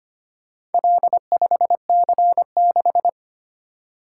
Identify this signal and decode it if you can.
Morse code, heard as a keyed tone, about 25 wpm, 700 Hz. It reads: L5C6
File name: practice / 25wpm / Four Letter-Number Combinations - 25wpm